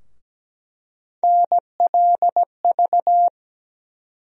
Morse code, heard as a keyed tone, about 17 wpm, 700 Hz. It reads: NLV
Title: USA Call Sign Postfixes - 3 Letters 17wpm